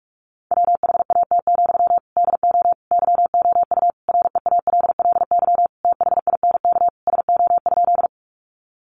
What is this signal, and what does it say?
Morse code, heard as a keyed tone, about 44 wpm, 700 Hz. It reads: WHAT, DO YOU REALLY THINK SO?
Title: Sentences from Top 600 Words 45wpm